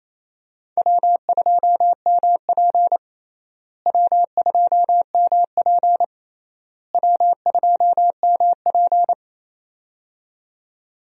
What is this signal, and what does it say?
Morse code, heard as a keyed tone, about 28 wpm, 700 Hz. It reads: W2MP W2MP W2MP
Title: USA Call Signs - 4 Characters - Repeated 3x - 28wpm